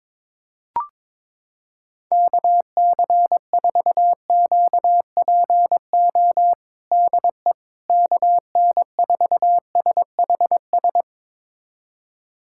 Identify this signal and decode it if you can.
Morse code, heard as a keyed tone, about 22 wpm, 700 Hz. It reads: KC4QPO DE KN4SHS
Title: USA Callsign DE Callsign 22wpm